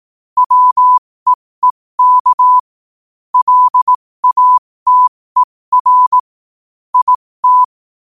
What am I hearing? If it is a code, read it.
Morse code, heard as a keyed tone, about 18 wpm, spaced as for 15 wpm, 1000 Hz. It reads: WEEK LATER IT
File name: qotdfiles / 15wpm